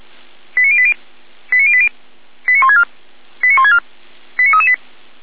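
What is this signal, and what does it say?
Five-tone selective call (ZVEI), heard as quick runs of five tones, tones of about 70 ms. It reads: 89080 89080 89164 89164 89338
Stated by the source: Five-tone ZVEI selective call received by SDR, from audio.f85360.feuerwehr-orlamuende.org, fire brigade Orlamünde.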